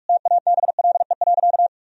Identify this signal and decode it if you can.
Morse code, heard as a keyed tone, about 44 wpm, 700 Hz. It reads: TABLE.
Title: Sentences from Top 1000 Words - Review Part 2 - 45wpm